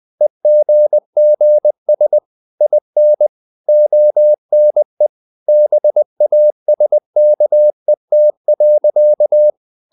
Morse code, heard as a keyed tone, about 20 wpm, 600 Hz. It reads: EGGS IN ONE BASKET.
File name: CW / Practice / EnglishSayings20